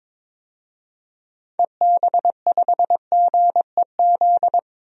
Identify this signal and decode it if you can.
Morse code, heard as a keyed tone, about 22 wpm, 700 Hz. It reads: EB5GEZ
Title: DXCC-100 Call Signs - 6 Characters - Repeated 2x - 22wpm